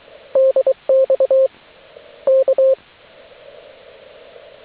Morse code, heard as a keyed tone, about 23 wpm, 530 Hz, with radio noise. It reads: DX K